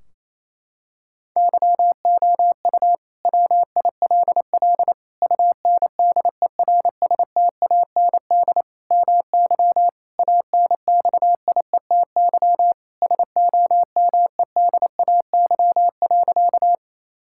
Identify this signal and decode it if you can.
Morse code, heard as a keyed tone, about 28 wpm, 700 Hz. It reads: YOU WILL UNDERSTAND MY ANXIETY SOMEDAY.
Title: Sentences from Top 1600 Words 28wpm